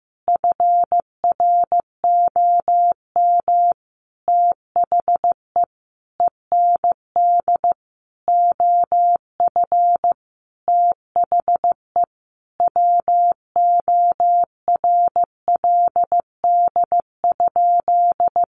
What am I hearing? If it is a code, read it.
Morse code, heard as a keyed tone, about 15 wpm, 700 Hz. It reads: FROM THE END OF THE WORLD?
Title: Sentences from Top 200 Words 15wpm